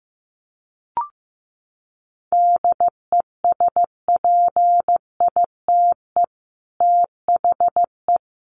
Morse code, heard as a keyed tone, about 15 wpm, 700 Hz. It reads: DESPITE THE
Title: Sets of 4 Words 15wpm